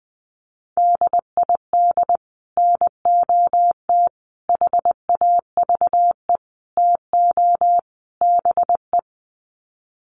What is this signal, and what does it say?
Morse code, heard as a keyed tone, about 20 wpm, 700 Hz. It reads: DID NOT HAVE TO BE
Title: Sets of 5 Words Encore2 20wpm